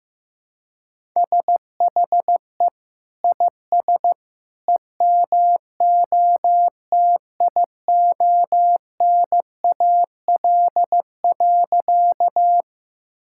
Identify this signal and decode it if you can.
Morse code, heard as a keyed tone, about 15 wpm, 700 Hz. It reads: SHE IS EMOTIONAL.
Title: Sentences from Top 1000 Words - Review Part 5 - 15wpm